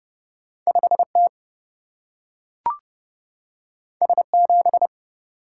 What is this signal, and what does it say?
Morse code, heard as a keyed tone, about 30 wpm, 700 Hz. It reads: 5 T S 7
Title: Single-Letter Single-Number 30wpm